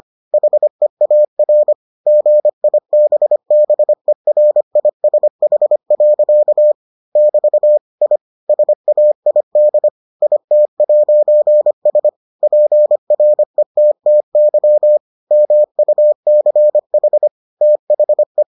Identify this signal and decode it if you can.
Morse code, heard as a keyed tone, about 25 wpm, 600 Hz. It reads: HEAR GIBBERISH. <BT> I SAID IT'S PRETTY MUCH THE